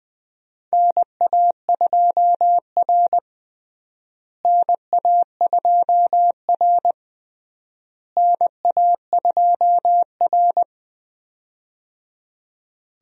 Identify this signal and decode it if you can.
Morse code, heard as a keyed tone, about 20 wpm, 700 Hz. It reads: NA2R NA2R NA2R
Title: USA Call Signs - 4 Characters - Repeated 3x - 20wpm